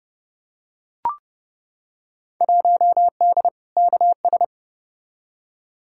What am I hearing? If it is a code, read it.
Morse code, heard as a keyed tone, about 32 wpm, 700 Hz. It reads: 1D KS
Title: Field Day Exchanges - 30wpm